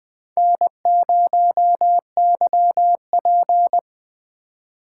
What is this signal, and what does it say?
Morse code, heard as a keyed tone, about 20 wpm, 700 Hz. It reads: N0YP
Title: USA Call Signs - 4 Characters - 4x Character Spacing - 20wpm